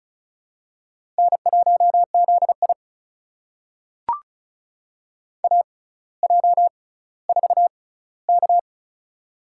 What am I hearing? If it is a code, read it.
Morse code, heard as a keyed tone, about 36 wpm, spaced as for 18 wpm, 700 Hz. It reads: N1ZI AJ4K
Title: USA Call Signs - 4 Characters - 6x Character Spacing - 35wpm